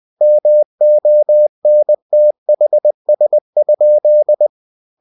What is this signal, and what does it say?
Morse code, heard as a keyed tone, about 20 wpm, 600 Hz. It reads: MONTHS?
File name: cw-books / winnie-the-pooh / winnie-20wpm-page-0065